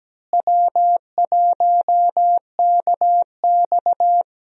Morse code, heard as a keyed tone, about 17 wpm, 700 Hz. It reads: W1KX
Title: USA Call Signs - 4 Characters - Repeated 2x - 17wpm